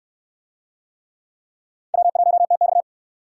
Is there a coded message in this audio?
Yes: Morse code from 1.9 to 2.8 s, heard as a keyed tone, about 62 wpm, 700 Hz. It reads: W1TX